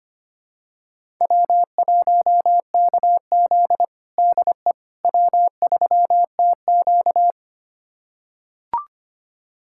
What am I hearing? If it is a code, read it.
Morse code, heard as a keyed tone, about 25 wpm, 700 Hz. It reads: W1KZ DE W3TQ
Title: USA Callsign DE Callsign - 4 Characters - 25wpm